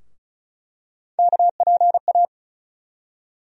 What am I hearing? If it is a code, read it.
Morse code, heard as a keyed tone, about 35 wpm, 700 Hz. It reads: KPA
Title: USA Call Sign Postfixes - 3 Letters 35wpm